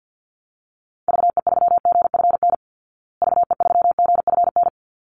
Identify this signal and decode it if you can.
Morse code, heard as a keyed tone, about 50 wpm, 700 Hz. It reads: VE3GFN VE3GFN
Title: Call Signs Repeated 2x - 50wpm